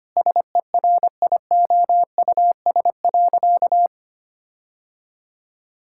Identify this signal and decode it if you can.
Morse code, heard as a keyed tone, about 25 wpm, 700 Hz. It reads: SERIOUS.
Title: Sentences from Top 1600 Words 25wpm